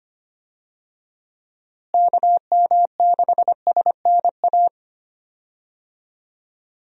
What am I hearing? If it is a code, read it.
Morse code, heard as a keyed tone, about 25 wpm, 700 Hz. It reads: KM6SNA
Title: USA Call Signs - 6 Characters - Repeated 3x - 25wpm